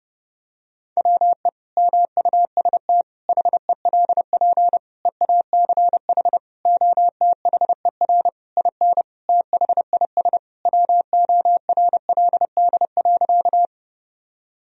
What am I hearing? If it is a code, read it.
Morse code, heard as a keyed tone, about 30 wpm, 700 Hz. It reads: WE MUST HELP EACH OTHER IN THIS WORLD.
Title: Sentences from Top 1000 Words - Review Part 1 - 30wpm